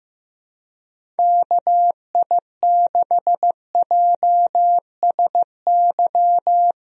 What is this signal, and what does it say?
Morse code, heard as a keyed tone, about 15 wpm, 700 Hz. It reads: KI6JSY
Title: USA Call Signs - 6 Characters - Repeated 2x - 15wpm